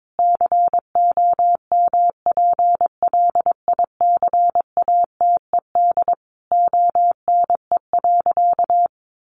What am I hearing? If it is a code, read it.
Morse code, heard as a keyed tone, about 22 wpm, 700 Hz. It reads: COMPLICATED ONE.
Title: Sentences from Top 1200 Words 22wpm